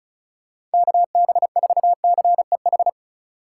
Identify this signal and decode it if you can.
Morse code, heard as a keyed tone, about 36 wpm, 700 Hz. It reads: KB4CEH
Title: USA Call Signs - 6 Characters - 4x Character Spacing - 35wpm